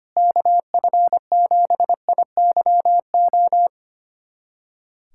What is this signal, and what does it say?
Morse code, heard as a keyed tone, about 25 wpm, 700 Hz. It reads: KF7IYO
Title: USA Callsign DE Callsign - 6 Characters - 25wpm